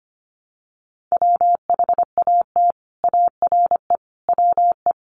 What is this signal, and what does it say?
Morse code, heard as a keyed tone, about 25 wpm, 700 Hz. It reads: WHAT ARE WE
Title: Sets of 5 Words 25wpm